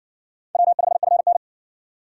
Morse code, heard as a keyed tone, about 60 wpm, 700 Hz. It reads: W5FN